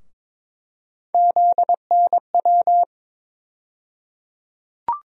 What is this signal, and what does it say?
Morse code, heard as a keyed tone, about 22 wpm, 700 Hz. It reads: ZNW
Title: USA Call Sign Postfixes - 3 Letters 22wpm